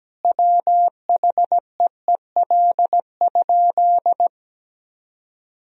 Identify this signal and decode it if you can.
Morse code, heard as a keyed tone, about 17 wpm, 700 Hz. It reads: WHEEL?
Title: Sentences from Top 1000 Words - Review Part 3 - 17wpm